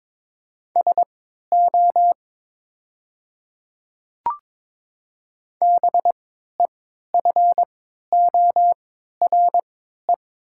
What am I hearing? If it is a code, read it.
Morse code, heard as a keyed tone, about 22 wpm, spaced as for 12 wpm, 690 Hz. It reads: SO BEFORE
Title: 3x Character Spacing - Top 200 Words - 22wpm